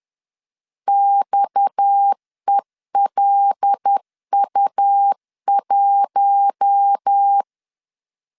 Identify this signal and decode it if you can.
Morse code, heard as a keyed tone, about 10 wpm, 800 Hz. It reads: XELU1
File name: LEZIONI / 9 2 10wpm 2736